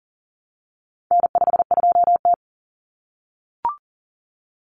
Morse code, heard as a keyed tone, about 40 wpm, 700 Hz. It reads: N52T
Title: Common Cut Numbers - 4-Digit 40wpm